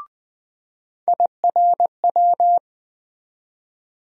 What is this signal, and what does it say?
Morse code, heard as a keyed tone, about 20 wpm, 700 Hz. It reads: IRW
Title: USA Call Sign Postfixes - 3 Letters 20wpm